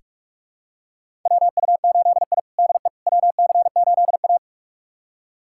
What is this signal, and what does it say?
Morse code, heard as a keyed tone, about 46 wpm, 700 Hz. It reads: WU8I DE WC8A
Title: USA Callsign DE Callsign - 4 Characters - 45wpm